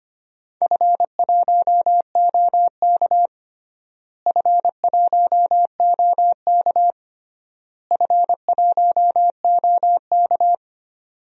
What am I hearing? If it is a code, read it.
Morse code, heard as a keyed tone, about 25 wpm, 700 Hz. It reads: F1OK F1OK F1OK
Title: DXCC-100 Call Signs - 4 Characters - Repeated 3x - 25wpm